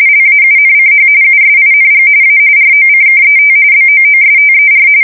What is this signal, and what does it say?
RTTY, about 45.45 baud. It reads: LAUNCH USING A ROCKET BOOSTER
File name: fsk170 / USS-New-Jersey